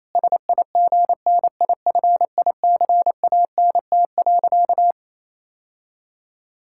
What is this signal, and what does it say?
Morse code, heard as a keyed tone, about 28 wpm, 700 Hz. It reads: SIGNIFICANT.
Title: Sentences from Top 1000 Words - Review Part 6 - 28wpm